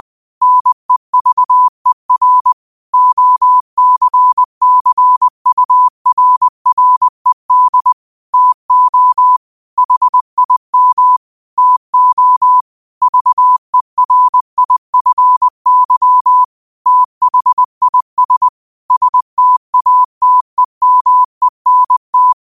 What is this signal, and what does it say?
Morse code, heard as a keyed tone, about 20 wpm, 1000 Hz. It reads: NEVER OCCURRED TO HIM TO VERIFY THIS STATEMENT
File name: qotdfiles / 20wpm